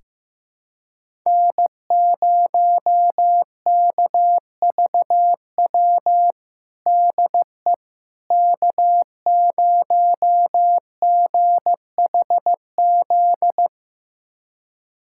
Morse code, heard as a keyed tone, about 15 wpm, 700 Hz. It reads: N0KVW DE K0GHZ
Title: USA Callsign DE Callsign - 5 Characters - 15wpm